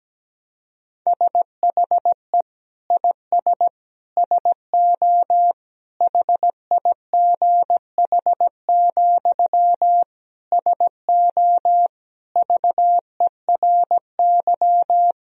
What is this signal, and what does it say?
Morse code, heard as a keyed tone, about 17 wpm, 700 Hz. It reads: SHE IS SO HIGH, SO VERY
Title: Sentences from Top 1000 Words - Review Part 1 - 17wpm